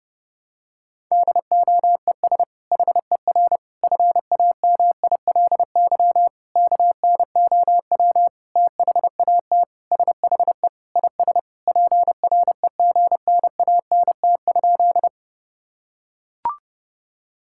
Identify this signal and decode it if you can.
Morse code, heard as a keyed tone, about 32 wpm, 700 Hz. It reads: DOES HER FAMILY KNOW THAT SHE IS PREGNANT?
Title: Sentences from Top 1000 Words - Review Part 6 - 30wpm